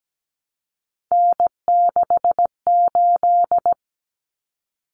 Morse code, heard as a keyed tone, about 17 wpm, 700 Hz. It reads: N68
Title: Common Cut Numbers - 3-Digit 17wpm